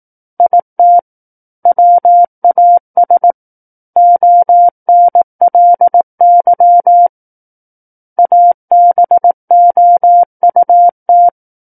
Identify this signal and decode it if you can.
Morse code, heard as a keyed tone, about 18 wpm, 700 Hz. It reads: IT WAS ONLY ABOUT
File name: 18 WPM